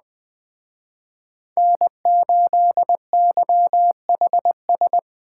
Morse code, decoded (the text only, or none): N8YHS